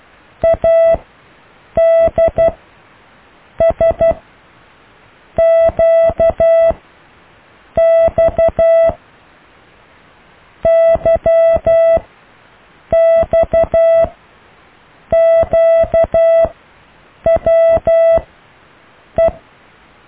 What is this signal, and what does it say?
Morse code, heard as a keyed tone, about 12 wpm, spaced as for 6 wpm, 660 Hz. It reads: ADSQX YXQWE
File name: cw / Alle letters